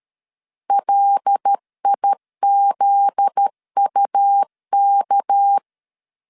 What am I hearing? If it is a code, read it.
Morse code, heard as a keyed tone, about 13 wpm, 800 Hz. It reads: LIZUK
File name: LEZIONI / 8 YJ45 / 8 YJ 12wpm 2235